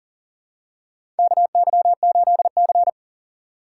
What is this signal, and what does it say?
Morse code, heard as a keyed tone, about 40 wpm, 700 Hz. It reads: KY8C
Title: USA Call Signs - 4x Character Spacing - 40wpm